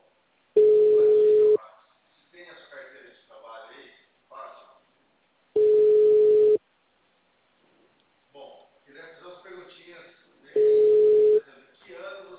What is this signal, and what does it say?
Ringback tone (Europe, 425 Hz), with speech.